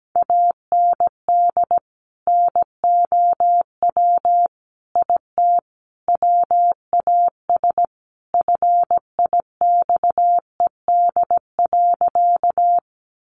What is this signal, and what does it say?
Morse code, heard as a keyed tone, about 17 wpm, 700 Hz. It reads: AND NOW IT WAS FIXED.
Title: Sentences from Top 1100 Words 17wpm